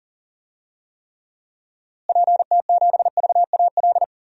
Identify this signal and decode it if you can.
Morse code, heard as a keyed tone, about 40 wpm, 700 Hz. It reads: PT7VAL
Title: DXCC-100 Call Signs - 6 Characters - Repeated 3x - 40wpm